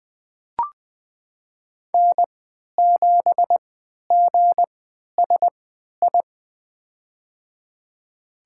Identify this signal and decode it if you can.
Morse code, heard as a keyed tone, about 20 wpm, spaced as for 11 wpm, 700 Hz. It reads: N7GSI
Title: USA Call Signs - 5 Characters - 3x Character Spacing - 20wpm